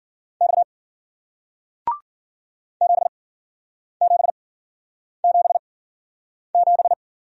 Morse code, heard as a keyed tone, about 54 wpm, spaced as for 9 wpm, 700 Hz. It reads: X 7777